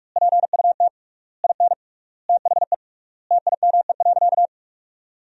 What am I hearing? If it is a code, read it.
Morse code, heard as a keyed tone, about 43 wpm, 700 Hz. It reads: PUT IN THE TIME.
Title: 3x Word-Spacing - Sentences from Top 200 Words - 45wpm